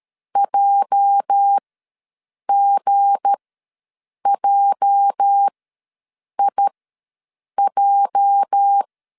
Morse code, heard as a keyed tone, about 13 wpm, spaced as for 7 wpm, 800 Hz. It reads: JGJIJ
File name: LEZIONI / 8 YJ45 / 8 YJ ws 12wpm 3342